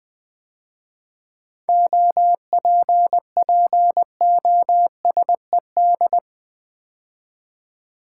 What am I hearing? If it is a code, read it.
Morse code, heard as a keyed tone, about 20 wpm, 700 Hz. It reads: OPPOSED
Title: 7 Letter Words - First 3 Letters Given - 20wpm